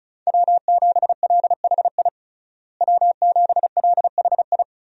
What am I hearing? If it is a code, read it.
Morse code, heard as a keyed tone, about 34 wpm, 700 Hz. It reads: W7LHI W7LHI
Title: USA Call Signs - 5 Characters - Repeated 3x - 35wpm